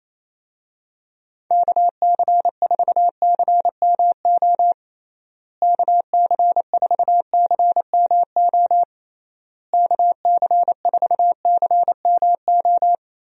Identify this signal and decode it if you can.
Morse code, heard as a keyed tone, about 28 wpm, 700 Hz. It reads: KC4CMO KC4CMO KC4CMO
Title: USA Call Signs - 6 Characters - Repeated 3x - 28wpm